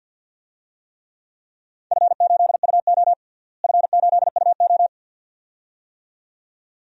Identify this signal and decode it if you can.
Morse code, heard as a keyed tone, about 52 wpm, 700 Hz. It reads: F8UO F8UO
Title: DXCC-100 Call Signs - 4 Characters - Repeated 2x - 50wpm